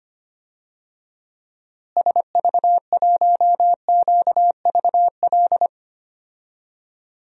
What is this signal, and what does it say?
Morse code, heard as a keyed tone, about 25 wpm, 700 Hz. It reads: SV1QVL
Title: DXCC-100 Call Signs - 6 Characters - Repeated 2x - 25wpm